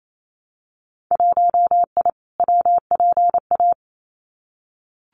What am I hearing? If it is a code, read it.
Morse code, heard as a keyed tone, about 28 wpm, 700 Hz. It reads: 1I WPA
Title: Winter Field Day Exchanges Encore 28wpm